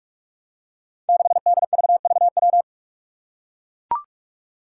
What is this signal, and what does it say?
Morse code, heard as a keyed tone, about 44 wpm, 700 Hz. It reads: 6DVVW
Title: Five Challenging Letter-Number Combinations 45wpm